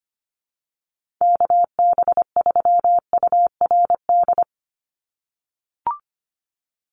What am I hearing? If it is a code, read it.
Morse code, heard as a keyed tone, about 25 wpm, 700 Hz. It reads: KB3URD
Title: USA Call Signs 25wpm